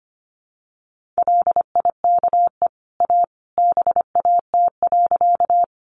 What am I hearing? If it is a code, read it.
Morse code, heard as a keyed tone, about 25 wpm, 700 Hz. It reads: LIKE A BAT.
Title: Sentences from Top 1500 Words 25wpm